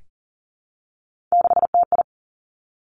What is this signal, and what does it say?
Morse code, heard as a keyed tone, about 38 wpm, 700 Hz. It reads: 6TI
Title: Three Challenging Letter-Number Combinations 40wpm